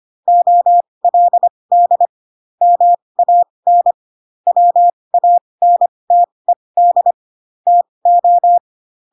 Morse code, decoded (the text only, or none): OLD MAN WANTED TO